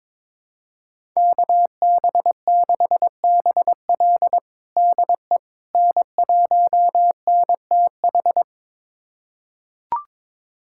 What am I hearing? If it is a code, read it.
Morse code, heard as a keyed tone, about 22 wpm, 700 Hz. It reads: KB6BL DE N1NTH
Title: USA Callsign DE Callsign - 5 Characters - 22wpm